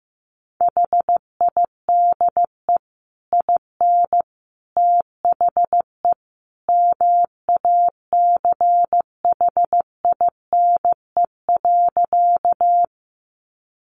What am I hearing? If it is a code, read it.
Morse code, heard as a keyed tone, about 15 wpm, 700 Hz. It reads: HIDE IN THE MACHINE.
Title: A Sound Of Thunder 15wpm